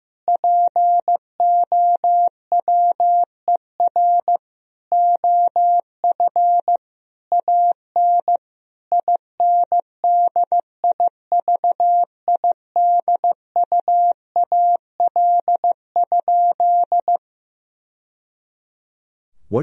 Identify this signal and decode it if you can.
Morse code, heard as a keyed tone, about 15 wpm, 700 Hz. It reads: POWER OF AN INDIVIDUAL?
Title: Sentences from Top 1000 Words - Review Part 4 - 15wpm